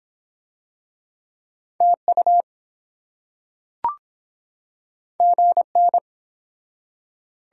Morse code, heard as a keyed tone, about 24 wpm, 700 Hz. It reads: TU GN